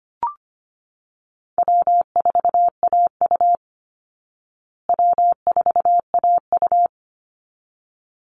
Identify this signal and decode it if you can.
Morse code, heard as a keyed tone, about 25 wpm, 700 Hz. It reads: W4AU W4AU
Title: Easy Call Signs Repeated 2x - 25wpm